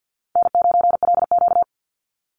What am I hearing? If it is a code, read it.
Morse code, heard as a keyed tone, about 50 wpm, 700 Hz. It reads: N9LQ
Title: Easy Call Signs - 50wpm